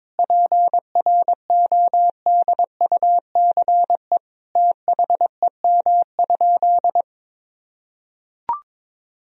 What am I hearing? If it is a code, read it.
Morse code, heard as a keyed tone, about 21 wpm, 700 Hz. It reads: PRODUCE THEM?
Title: Sentences from Top 1000 Words - Review Part 3 - 22wpm